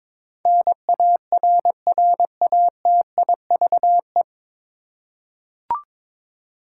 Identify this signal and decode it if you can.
Morse code, heard as a keyed tone, about 22 wpm, 700 Hz. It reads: NARRATIVE
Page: Morse Code Ninja practice 22wpm